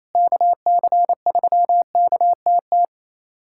Morse code, heard as a keyed tone, about 28 wpm, 700 Hz. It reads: KC3KTT
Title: USA Callsign DE Callsign 28wpm